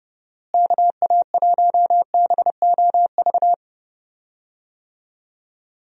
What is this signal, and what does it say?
Morse code, heard as a keyed tone, about 30 wpm, 700 Hz. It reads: KA1BOV
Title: USA Call Signs - 6 Characters - Repeated 3x - 30wpm